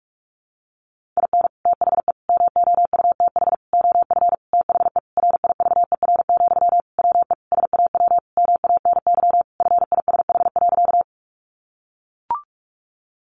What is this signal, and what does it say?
Morse code, heard as a keyed tone, about 44 wpm, 700 Hz. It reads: IN THE MOUTH OF THE RIVER, WE SAW MANY FISH.
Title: Sentences from Top 1500 Words 45wpm